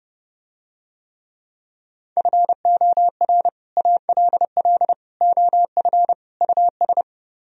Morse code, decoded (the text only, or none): FOR ALL OF US